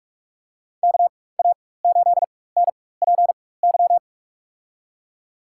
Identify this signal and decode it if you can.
Morse code, heard as a keyed tone, about 46 wpm, spaced as for 21 wpm, 700 Hz. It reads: KA8NPY
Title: USA Call Signs - 6 Characters - 4x Character Spacing - 45wpm